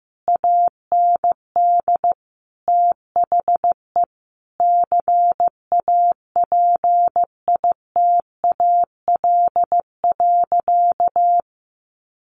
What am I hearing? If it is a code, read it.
Morse code, heard as a keyed tone, about 15 wpm, 700 Hz. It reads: AND THE CAPITAL.
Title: Sentences from Top 1500 Words 15wpm